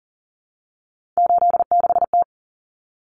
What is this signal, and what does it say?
Morse code, heard as a keyed tone, about 40 wpm, 700 Hz. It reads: 86T
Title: Common Cut Numbers - 3-Digit 40wpm